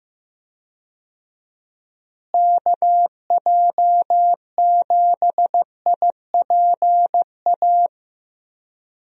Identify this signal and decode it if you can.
Morse code, heard as a keyed tone, about 16 wpm, 700 Hz. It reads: KJ7IPA
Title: USA Call Signs - 6 Characters - Repeated 2x - 15wpm